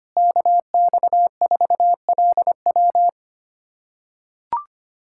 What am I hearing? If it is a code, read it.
Morse code, heard as a keyed tone, about 25 wpm, 700 Hz. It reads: KX4LW
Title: USA Callsign DE Callsign - 5 Characters - 25wpm